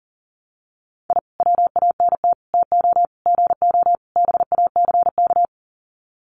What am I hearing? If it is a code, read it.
Morse code, heard as a keyed tone, about 40 wpm, 700 Hz. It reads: I WANT TO GO BACK
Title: Sets of 5 Words Encore 40wpm